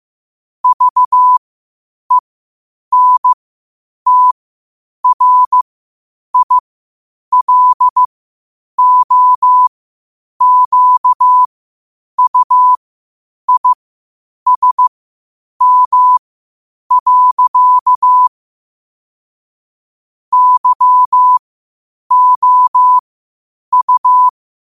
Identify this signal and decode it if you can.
Morse code, heard as a keyed tone, about 15 wpm, spaced as for 8 wpm, 1000 Hz. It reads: VENTRILOQUISM. YOU
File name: qotdfiles / 8wpm